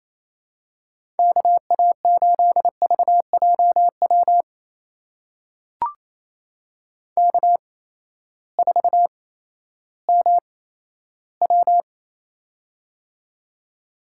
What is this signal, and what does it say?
Morse code, heard as a keyed tone, about 27 wpm, spaced as for 14 wpm, 700 Hz. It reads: KA8VJW K4MW
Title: USA Call Signs - 8x Character Spacing - 28wpm